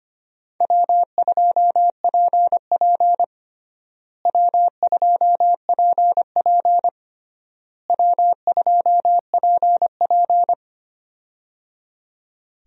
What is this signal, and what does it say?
Morse code, heard as a keyed tone, about 25 wpm, 700 Hz. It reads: W2PP W2PP W2PP
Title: USA Call Signs - 4 Characters - Repeated 3x - 25wpm